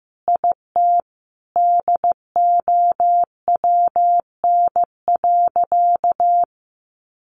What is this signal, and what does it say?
Morse code, heard as a keyed tone, about 15 wpm, 700 Hz. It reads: IT DOWN.
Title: Sentences from Top 1200 Words 15wpm